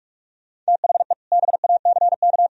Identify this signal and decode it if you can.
Morse code, heard as a keyed tone, about 45 wpm, 700 Hz. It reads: THE BACK
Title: Speed-Racing - Sets of 3 Words 45wpm